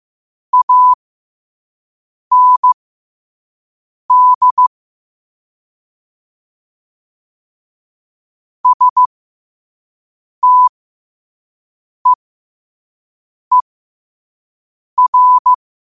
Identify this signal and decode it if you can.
Morse code, heard as a keyed tone, about 15 wpm, spaced as for 5 wpm, 1000 Hz. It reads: AND STEER
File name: qotdfiles / 5wpm